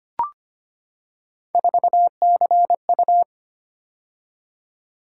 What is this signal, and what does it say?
Morse code, heard as a keyed tone, about 25 wpm, 700 Hz. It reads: 4CU
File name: Three Letter-Number Combinations - 25wpm